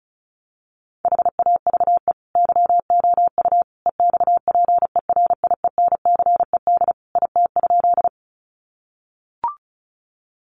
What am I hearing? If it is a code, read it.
Morse code, heard as a keyed tone, about 34 wpm, 700 Hz. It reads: HAVE YOU EXPERIENCED IT?
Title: Sentences from Top 1100 Words 35wpm